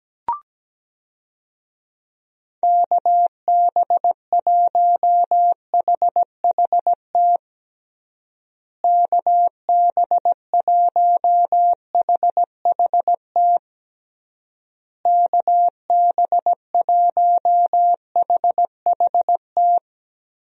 Morse code, heard as a keyed tone, about 17 wpm, 700 Hz. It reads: KB1HHT KB1HHT KB1HHT
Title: USA Call Signs - 6 Characters - Repeated 3x - 17wpm